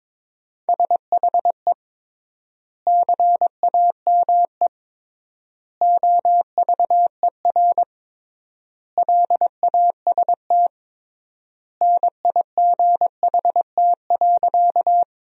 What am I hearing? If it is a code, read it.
Morse code, heard as a keyed tone, about 22 wpm, 700 Hz. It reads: SHE CAME OVER LAST NIGHT.
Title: 3x Word-Spacing - Sentences from Top 300 Words - 22wpm